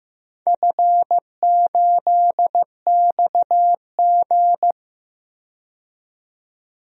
Morse code, heard as a keyed tone, about 15 wpm, 700 Hz. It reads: F8XG
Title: DXCC-100 Call Signs - 4 Characters - 15wpm